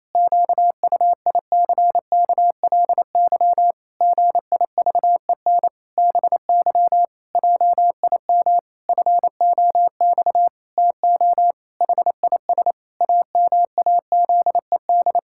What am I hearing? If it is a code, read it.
Morse code, heard as a keyed tone, about 28 wpm, 700 Hz. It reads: QUICKLY GIVEN BY JIM FOX TO HIS AMAZED
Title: Pangram Challenge v2 - 28wpm